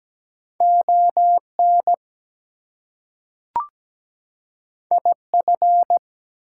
Morse code, heard as a keyed tone, about 17 wpm, 700 Hz. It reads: ON IF
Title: Two Letter Words - 17wpm